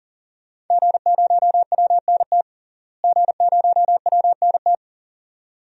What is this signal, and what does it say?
Morse code, heard as a keyed tone, about 40 wpm, 700 Hz. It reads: G0WNT G0WNT
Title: DXCC-100 Call Signs - 5 Characters - Repeated 2x - 40wpm